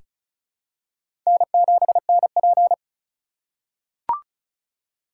Morse code, heard as a keyed tone, about 36 wpm, 700 Hz. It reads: N7NP
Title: USA Call Signs - 4 Characters - 2x Character Spacing - 35wpm